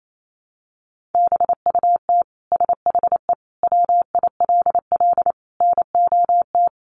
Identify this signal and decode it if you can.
Morse code, heard as a keyed tone, about 28 wpm, 700 Hz. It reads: BUT SHE WILL NOT
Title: Sentences from Top 1200 Words 28wpm